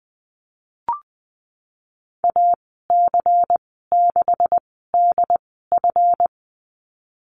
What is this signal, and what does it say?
Morse code, heard as a keyed tone, about 20 wpm, spaced as for 15 wpm, 700 Hz. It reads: AC6DF
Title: 167 Lesson 27g - Number 7 - Call Signs 20wpm